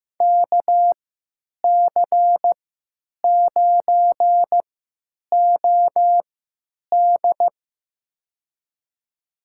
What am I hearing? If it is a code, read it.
Morse code, heard as a keyed tone, about 15 wpm, spaced as for 9 wpm, 700 Hz. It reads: KC9OD
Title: USA Call Signs - 5 Characters - 3x Character Spacing - 15wpm